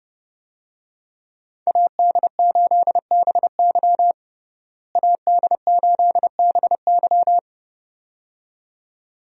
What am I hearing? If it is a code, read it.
Morse code, heard as a keyed tone, about 32 wpm, 700 Hz. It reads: AD8BY AD8BY
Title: USA Call Signs - 5 Characters - Repeated 2x - 30wpm